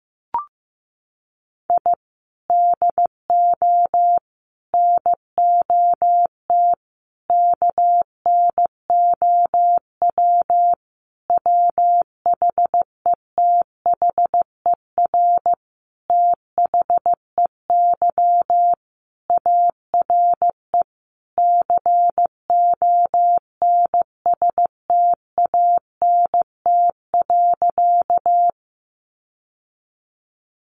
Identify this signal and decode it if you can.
Morse code, heard as a keyed tone, about 15 wpm, 700 Hz. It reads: I DO NOT KNOW WHETHER THEY ARE CONSTANT.
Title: Sentences from Top 1500 Words 15wpm